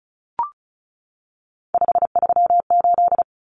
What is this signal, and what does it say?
Morse code, heard as a keyed tone, about 34 wpm, 700 Hz. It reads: 538
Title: Common Cut Numbers - 3-Digit 35wpm